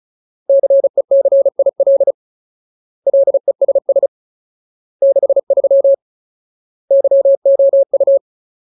Morse code, heard as a keyed tone, about 34 wpm, 550 Hz. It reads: CECIL LESS 63 YOU
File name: Ultimate Word List / 35WPM Ultimate